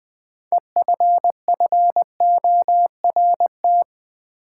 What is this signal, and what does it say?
Morse code, heard as a keyed tone, about 20 wpm, 700 Hz. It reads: EFFORT
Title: Sentences from Top 1200 Words - New Word Drill - 20wpm